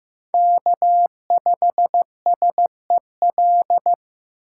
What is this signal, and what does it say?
Morse code, heard as a keyed tone, about 15 wpm, 700 Hz. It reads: K5SEL
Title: USA Call Signs - 2x Character Spacing - 15wpm